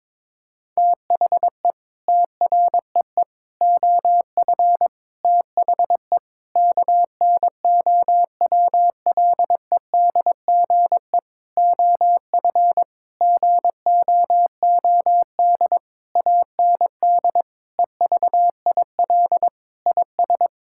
Morse code, heard as a keyed tone, about 22 wpm, 700 Hz. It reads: THE TREE OF THE KNOWLEDGE OF GOOD AND EVIL IS
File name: Sentences with 9-Word Phrases - 22wpm